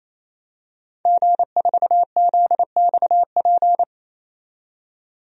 Morse code, heard as a keyed tone, about 28 wpm, 700 Hz. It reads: G4ZXP
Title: DXCC-100 Call Signs - 5 Characters - 28wpm